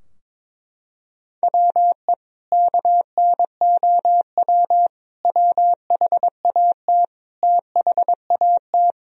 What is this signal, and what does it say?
Morse code, heard as a keyed tone, about 22 wpm, 700 Hz. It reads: WE KNOW WHAT THAT
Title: Sentences from Top 1000 Words - Review Part 4 - 22wpm